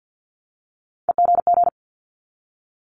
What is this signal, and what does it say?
Morse code, heard as a keyed tone, about 50 wpm, 700 Hz. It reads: EGG
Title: Multi-letter Phonemes Repeated 3x 50wpm